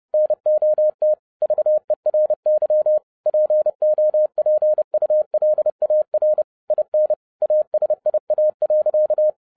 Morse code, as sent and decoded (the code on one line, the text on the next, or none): -. --- - / ...- . .-. -.-- / .--. --- .--. ..- .-.. .- .-. / .. -. / .- ... .. .- .-.-.-
NOT VERY POPULAR IN ASIA.